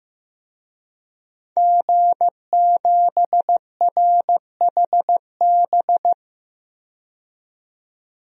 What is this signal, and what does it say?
Morse code, heard as a keyed tone, about 15 wpm, 700 Hz. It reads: G7RHB
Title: DXCC-100 Call Signs - 5 Characters - 15wpm